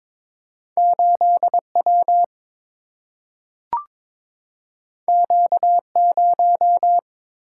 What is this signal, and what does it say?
Morse code, heard as a keyed tone, about 22 wpm, 700 Hz. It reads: 8 W Q 0
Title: Single-Letter Single-Number 22wpm